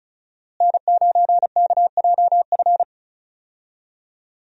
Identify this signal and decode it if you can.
Morse code, heard as a keyed tone, about 36 wpm, 700 Hz. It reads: N9KJF
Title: USA Call Signs - 5 Characters - Repeated 3x - 35wpm